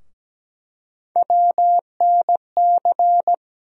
Morse code, heard as a keyed tone, about 17 wpm, 700 Hz. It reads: WNC
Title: USA Call Sign Postfixes - 3 Letters 17wpm